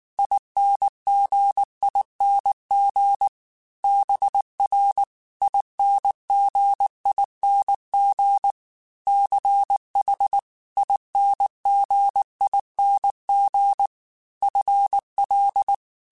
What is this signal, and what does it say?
Morse code, heard as a keyed tone, about 19 wpm, 800 Hz. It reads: INGING BRINGING CHINGING FL
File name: Special INGs-mp3s / Special INGs-mp3s / Dble Ings sp @18